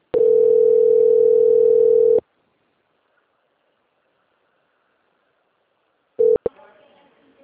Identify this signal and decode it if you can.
Ringback tone (North America), 440 + 480 Hz, with speech.
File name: rg-1600-unknown- - - .9620